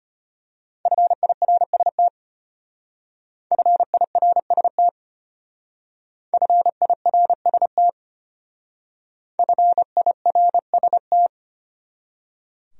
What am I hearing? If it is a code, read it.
Morse code, heard as a keyed tone, about 32 wpm, 700 Hz. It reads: FIRST FIRST FIRST FIRST